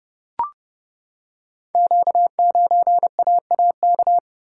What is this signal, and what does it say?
Morse code, heard as a keyed tone, about 30 wpm, 700 Hz. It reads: Q9AAK
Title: Code Groups - 5 Letter:Numbers - 30wpm